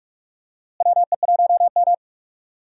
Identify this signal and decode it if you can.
Morse code, heard as a keyed tone, about 46 wpm, 700 Hz. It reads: WE1M
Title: USA Call Signs - 4 Characters - Repeated 3x - 45wpm